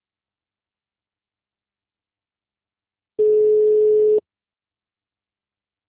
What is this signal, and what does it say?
Ringback tone (Europe, 425 Hz).